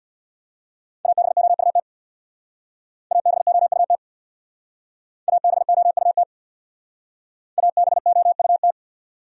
Morse code, heard as a keyed tone, about 64 wpm, 700 Hz. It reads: ABOUT ABOUT ABOUT ABOUT